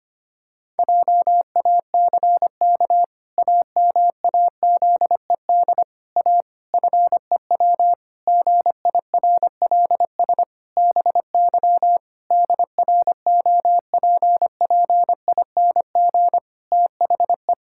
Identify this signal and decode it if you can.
Morse code, heard as a keyed tone, about 25 wpm, 700 Hz. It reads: JACK AMAZED A FEW GIRLS BY DROPPING THE
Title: Pangram Challenge v2 - 25wpm